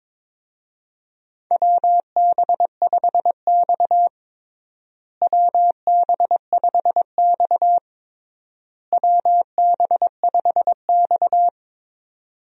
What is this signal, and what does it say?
Morse code, heard as a keyed tone, about 22 wpm, 700 Hz. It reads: WB5X WB5X WB5X
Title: USA Call Signs - 4 Characters - Repeated 3x - 22wpm